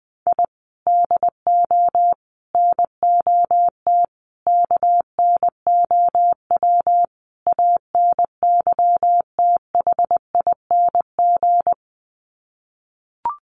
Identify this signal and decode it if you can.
Morse code, heard as a keyed tone, about 20 wpm, 700 Hz. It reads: I DO NOT KNOW ANYTHING
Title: Sets of 5 Words Encore 20wpm